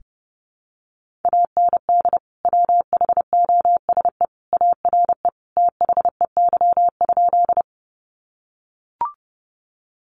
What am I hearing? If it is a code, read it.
Morse code, heard as a keyed tone, about 30 wpm, 700 Hz. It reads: AND WHOSE ARE THEY?
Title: Sentences from Top 1200 Words 30wpm